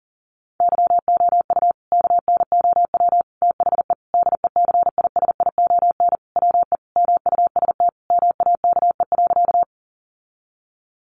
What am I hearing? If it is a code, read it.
Morse code, heard as a keyed tone, about 40 wpm, 700 Hz. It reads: YOU KNOW THE DECISION WE MUST MAKE.
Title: Sentences from Top 1000 Words 40wpm